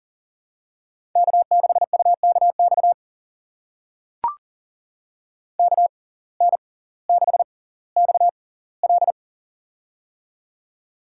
Morse code, heard as a keyed tone, about 39 wpm, spaced as for 20 wpm, 700 Hz. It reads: K6UKX KN6XL
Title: USA Call Signs - 5 Characters - 6x Character Spacing - 40wpm